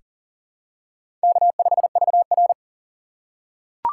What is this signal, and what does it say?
Morse code, heard as a keyed tone, about 39 wpm, 700 Hz. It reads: K5VR
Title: K1USN SST QSO Practice - 40wpm